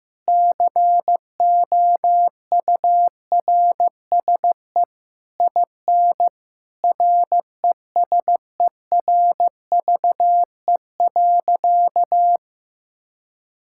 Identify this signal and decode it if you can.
Morse code, heard as a keyed tone, about 16 wpm, 700 Hz. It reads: COURSE IN RESERVE.
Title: Sentences from Top 1000 Words - Review Part 6 - 15wpm